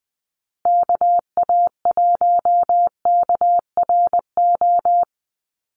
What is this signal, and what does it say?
Morse code, heard as a keyed tone, about 20 wpm, 700 Hz. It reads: KA1KRO
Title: USA Call Signs Repeated 2x 20wpm